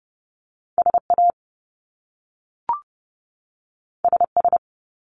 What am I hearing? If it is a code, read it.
Morse code, heard as a keyed tone, about 30 wpm, 700 Hz. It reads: SA SS
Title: Top Two Letter Combinations in Words 30wpm